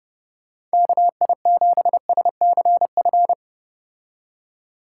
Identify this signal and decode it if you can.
Morse code, heard as a keyed tone, about 34 wpm, 700 Hz. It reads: KI7SCF